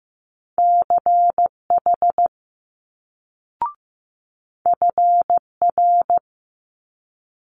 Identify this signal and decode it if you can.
Morse code, heard as a keyed tone, about 15 wpm, 700 Hz. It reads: CH FR
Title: Multi-letter Phonemes Quick Fire 15wpm